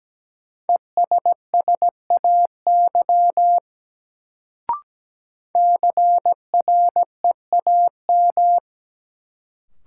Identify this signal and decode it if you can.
Morse code, heard as a keyed tone, about 17 wpm, 700 Hz. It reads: ESSAY CREAM
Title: Sentences from Top 1500 Words - New Word Drill Alt - 17wpm